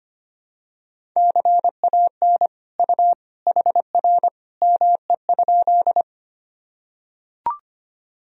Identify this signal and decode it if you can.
Morse code, heard as a keyed tone, about 25 wpm, 700 Hz. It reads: CAN U HR ME?
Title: Sentences from Top 300 Words and Abbreviations 25wpm